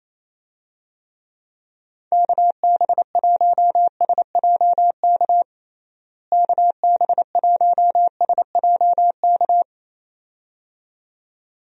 Morse code, heard as a keyed tone, about 28 wpm, 700 Hz. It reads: KB1SJK KB1SJK
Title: USA Call Signs - 6 Characters - Repeated 2x - 28wpm